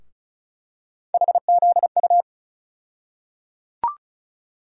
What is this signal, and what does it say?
Morse code, heard as a keyed tone, about 36 wpm, 700 Hz. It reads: HZU
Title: USA Call Sign Postfixes - 3 Letters 35wpm